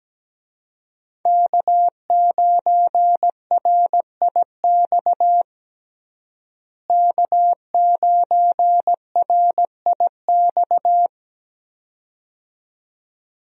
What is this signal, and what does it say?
Morse code, heard as a keyed tone, about 17 wpm, 700 Hz. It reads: K9RIX K9RIX
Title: USA Call Signs - 5 Characters - Repeated 2x - 17wpm